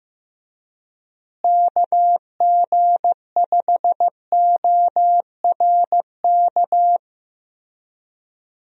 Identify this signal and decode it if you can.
Morse code, heard as a keyed tone, about 15 wpm, 700 Hz. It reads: KG5ORK
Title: USA Call Signs - 6 Characters - Repeated 2x - 15wpm